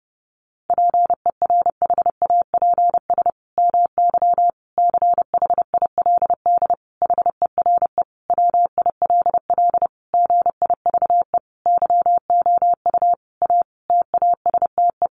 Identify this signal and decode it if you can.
Morse code, heard as a keyed tone, about 30 wpm, 700 Hz. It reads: PERHAPS MY CHILD HERE WILL GIVE YOU A TASTE
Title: Sentences from Top 1100 Words 30wpm